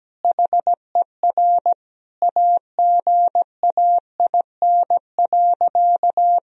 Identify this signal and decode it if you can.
Morse code, heard as a keyed tone, about 17 wpm, 700 Hz. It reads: HER AGAIN.
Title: Sentences from Top 1000 Words - Review Part 4 - 17wpm